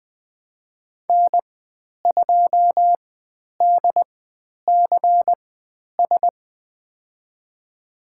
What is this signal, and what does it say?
Morse code, heard as a keyed tone, about 20 wpm, spaced as for 10 wpm, 700 Hz. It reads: N2DCS
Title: 102 Lesson 17g - Letter u - Call Signs 20wpm